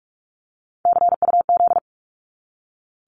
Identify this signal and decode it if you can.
Morse code, heard as a keyed tone, about 46 wpm, 700 Hz. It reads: CUZ
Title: Abbreviations 45wpm